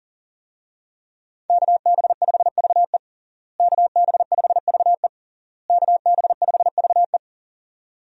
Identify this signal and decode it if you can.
Morse code, heard as a keyed tone, about 37 wpm, 700 Hz. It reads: KB5VE KB5VE KB5VE
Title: USA Call Signs - 5 Characters - Repeated 3x - 40wpm